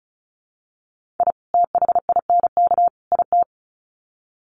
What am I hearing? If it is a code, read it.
Morse code, heard as a keyed tone, about 34 wpm, 700 Hz. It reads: I THINK IT
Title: Sets of 3 Words 35wpm